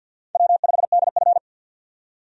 Morse code, heard as a keyed tone, about 52 wpm, 700 Hz. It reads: W5DF
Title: USA Call Signs - 4 Characters - 50wpm